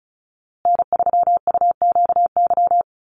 Morse code, heard as a keyed tone, about 34 wpm, 700 Hz. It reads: N3UQY
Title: USA Call Signs Repeated 3x 35wpm